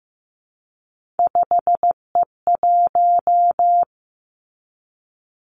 Morse code, heard as a keyed tone, about 15 wpm, 700 Hz. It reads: 5E1
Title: Three Challenging Letter-Number Combinations 15wpm